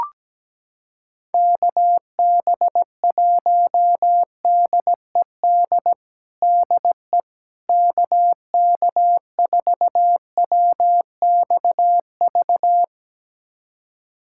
Morse code, heard as a keyed tone, about 17 wpm, 700 Hz. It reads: KB1DED DE KK4WXV
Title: USA Callsign DE Callsign - 6 Characters - 17wpm